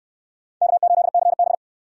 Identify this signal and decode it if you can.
Morse code, heard as a keyed tone, about 62 wpm, 700 Hz. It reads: K8CB